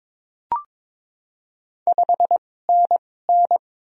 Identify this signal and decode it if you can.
Morse code, heard as a keyed tone, about 21 wpm, spaced as for 16 wpm, 700 Hz. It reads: 5NN
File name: Top 20 QSO Elements - 2x Character Spacing - 22wpm